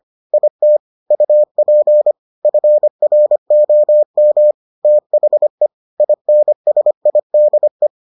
Morse code, heard as a keyed tone, about 25 wpm, 600 Hz. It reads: IT UP FROM THE INSIDE